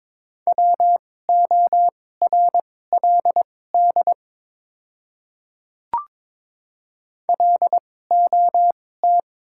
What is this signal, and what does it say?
Morse code, heard as a keyed tone, about 22 wpm, spaced as for 16 wpm, 700 Hz. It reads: WORLD LOT
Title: 2x Character Spacing - Top 500 Words - 22wpm